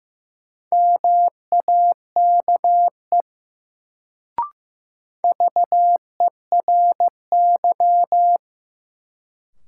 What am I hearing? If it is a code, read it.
Morse code, heard as a keyed tone, about 15 wpm, 700 Hz. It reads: MAKE VERY
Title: Sets of 2 Words - New Word Drill - 15wpm